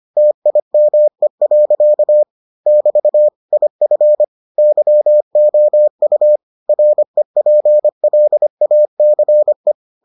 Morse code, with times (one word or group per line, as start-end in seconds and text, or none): 0.2-2.2 TIME.
2.7-3.3 <BT>
3.5-4.2 IF
4.6-6.4 YOU
6.7-9.7 REPLACE